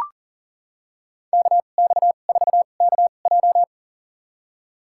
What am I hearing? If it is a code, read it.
Morse code, heard as a keyed tone, about 40 wpm, spaced as for 29 wpm, 700 Hz. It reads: KX4KJ
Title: USA Call Signs - 2x Character Spacing - 40wpm